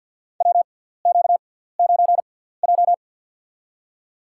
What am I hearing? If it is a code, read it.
Morse code, heard as a keyed tone, about 50 wpm, spaced as for 17 wpm, 700 Hz. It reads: WQ9J
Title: USA Call Signs - 4 Characters - 6x Character Spacing - 50wpm